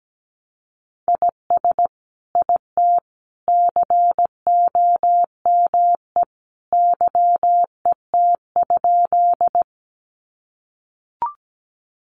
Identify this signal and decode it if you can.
Morse code, heard as a keyed tone, about 17 wpm, 700 Hz. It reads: IS IT COME YET?
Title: Sentences from Top 500 Words 17wpm